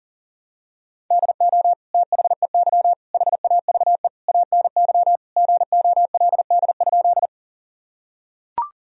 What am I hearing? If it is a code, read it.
Morse code, heard as a keyed tone, about 40 wpm, 700 Hz. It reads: DO THEY HAVE ANY GOLD?
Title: Sentences from Top 1000 Words - Review Part 3 - 40wpm